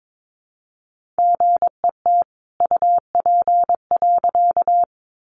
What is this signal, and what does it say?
Morse code, heard as a keyed tone, about 22 wpm, 700 Hz. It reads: GET UP.
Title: A Sound Of Thunder 22wpm